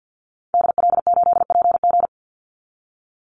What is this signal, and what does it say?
Morse code, heard as a keyed tone, about 51 wpm, 700 Hz. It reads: DL8PG
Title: Call Signs Repeated 2x - 50wpm